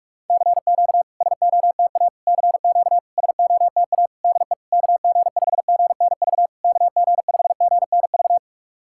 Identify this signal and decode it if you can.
Morse code, heard as a keyed tone, about 44 wpm, 690 Hz. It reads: CQ SOTA CQ SOTA DE KG5GNV KG5GNV
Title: SOTA QSO Practice 45wpm